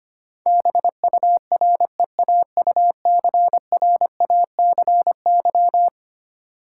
Morse code, heard as a keyed tone, about 25 wpm, 700 Hz. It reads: BUREAUCRACY
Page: morse-code-ninja-practice-25wpm